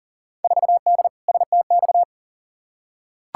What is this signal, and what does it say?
Morse code, heard as a keyed tone, about 38 wpm, 700 Hz. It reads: 4D STX
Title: Field Day Exchanges Encore - 40wpm